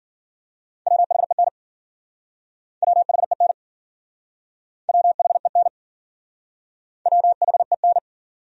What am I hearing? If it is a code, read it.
Morse code, heard as a keyed tone, about 53 wpm, 700 Hz. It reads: WHEN WHEN WHEN WHEN